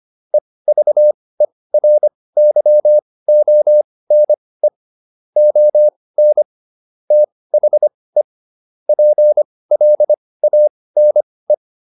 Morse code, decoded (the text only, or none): EVERYONE ON THE PLANE